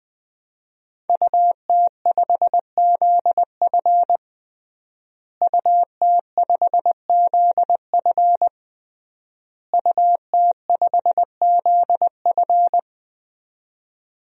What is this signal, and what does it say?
Morse code, heard as a keyed tone, about 20 wpm, 700 Hz. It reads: UT5ZF UT5ZF UT5ZF
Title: DXCC-100 Call Signs - 5 Characters - Repeated 3x - 20wpm